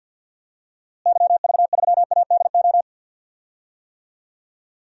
Morse code, heard as a keyed tone, about 49 wpm, 700 Hz. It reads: YV3ADO
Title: DXCC-100 Call Signs - 6 Characters - Repeated 3x - 50wpm